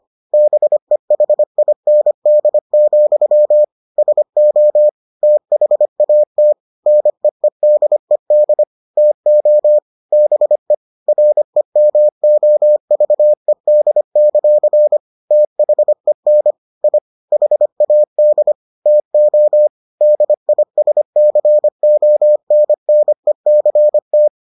Morse code, heard as a keyed tone, about 25 wpm, 600 Hz. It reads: BEHIND, SO THAT NEEDED TO BE REMOVED; THEN I HAD TO DISCONNECT